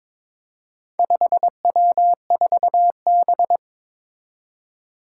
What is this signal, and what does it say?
Morse code, heard as a keyed tone, about 22 wpm, 700 Hz. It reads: 5W4B
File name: practice / 22wpm / Four Letter-Number Combinations - 22wpm